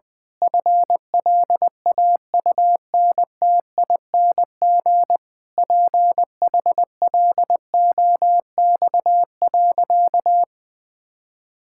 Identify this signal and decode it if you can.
Morse code, heard as a keyed tone, about 19 wpm, 700 Hz. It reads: FLAUNTING PHLOX.
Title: Pangram Challenge v2 - 20wpm